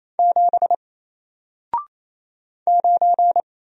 Morse code, heard as a keyed tone, about 28 wpm, 700 Hz. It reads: 7 9
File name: Single-Letter-Number-28wpm